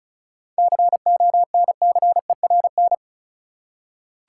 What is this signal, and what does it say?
Morse code, heard as a keyed tone, about 36 wpm, 700 Hz. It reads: CONCERN
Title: Seven Letter Words - 35wpm